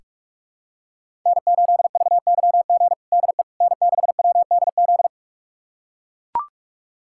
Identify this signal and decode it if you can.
Morse code, heard as a keyed tone, about 42 wpm, 700 Hz. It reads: N8VYG DE N6WDZ
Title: USA Callsign DE Callsign - 5 Characters - 45wpm